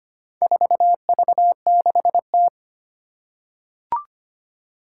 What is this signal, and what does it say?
Morse code, heard as a keyed tone, about 25 wpm, 700 Hz. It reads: 4V6T
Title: Four Challenging Letter-Number Combinations 25wpm